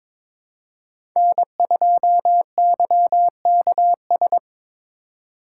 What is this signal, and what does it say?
Morse code, heard as a keyed tone, about 22 wpm, 700 Hz. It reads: N2YKS
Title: USA Call Signs - 5 Characters - 4x Character Spacing - 22wpm